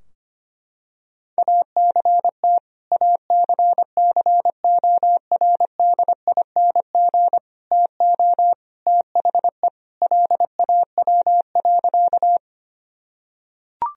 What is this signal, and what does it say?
Morse code, heard as a keyed tone, about 25 wpm, 700 Hz. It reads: ACT ACCORDING TO THE LAW.
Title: Sentences from Top 1000 Words - Review Part 4 - 25wpm